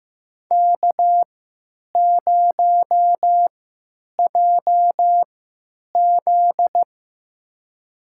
Morse code, heard as a keyed tone, about 15 wpm, spaced as for 9 wpm, 700 Hz. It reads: K0JZ
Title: USA Call Signs - 4 Characters - 3x Character Spacing - 15wpm